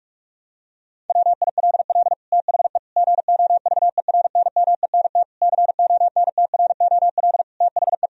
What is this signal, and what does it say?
Morse code, heard as a keyed tone, about 41 wpm, 700 Hz. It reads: WILL THE GOVERNMENT CONTROL THE
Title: Sentences from Top 1000 Words - Review Part 3 - 45wpm